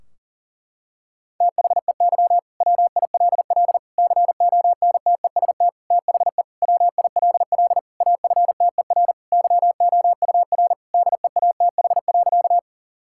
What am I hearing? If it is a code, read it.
Morse code, heard as a keyed tone, about 40 wpm, 700 Hz. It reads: THEY WILL CONTEST THE WILL AFTER YOUR DEATH.
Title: Sentences from Top 1600 Words 40wpm